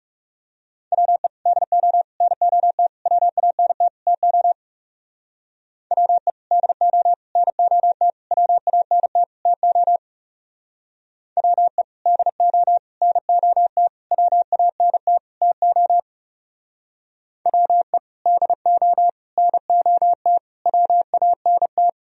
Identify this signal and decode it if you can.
Morse code, heard as a keyed tone, about 37 wpm, 700 Hz. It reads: WE DO NOT WANT TO WE DO NOT WANT TO WE DO NOT WANT TO WE DO NOT WANT
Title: Speed-Racing - Sets of 5 Words 30wpm